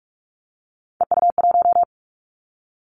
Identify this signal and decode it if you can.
Morse code, heard as a keyed tone, about 46 wpm, 700 Hz. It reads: EU1
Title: Callsign Prefixes for Top 50 DXCC - 45wpm